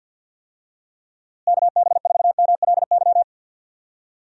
Morse code, heard as a keyed tone, about 52 wpm, 700 Hz. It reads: KB4MLY
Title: USA Call Signs - 6 Characters - Repeated 2x - 50wpm